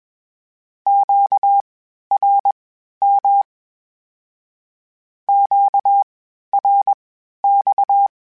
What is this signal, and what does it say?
Morse code, heard as a keyed tone, about 21 wpm, spaced as for 12 wpm, 800 Hz. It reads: QRM QRX
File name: (spaced) / Q-Calls Spaced-mp3s / Q Calls Twice sp @20